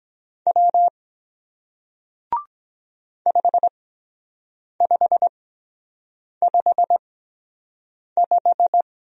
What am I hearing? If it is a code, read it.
Morse code, heard as a keyed tone, about 24 wpm, spaced as for 7 wpm, 700 Hz. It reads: W 5555